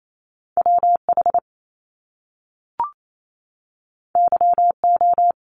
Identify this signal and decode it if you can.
Morse code, heard as a keyed tone, about 28 wpm, 700 Hz. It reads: WH YO
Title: Top Two Letter Combinations in Words 28wpm